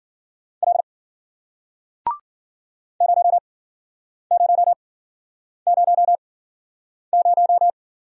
Morse code, heard as a keyed tone, about 55 wpm, spaced as for 9 wpm, 700 Hz. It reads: L 0000